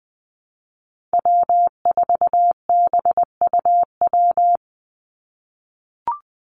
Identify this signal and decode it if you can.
Morse code, heard as a keyed tone, about 20 wpm, 700 Hz. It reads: W4BUW
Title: Call Signs - 20wpm